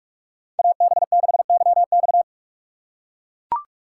Morse code, heard as a keyed tone, about 44 wpm, 700 Hz. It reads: AB6YX